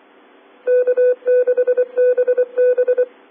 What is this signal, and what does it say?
Morse code, heard as a keyed tone, about 24 wpm, 1500 Hz. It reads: K6BB